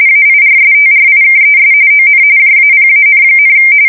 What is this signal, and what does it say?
RTTY, about 45.45 baud. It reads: AIRSTRIKES ON JAPAN,
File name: fsk170 / battleshipnj